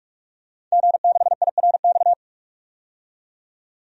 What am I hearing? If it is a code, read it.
Morse code, heard as a keyed tone, about 42 wpm, 700 Hz. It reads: G6IRX
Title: DXCC-100 Call Signs - 5 Characters - 45wpm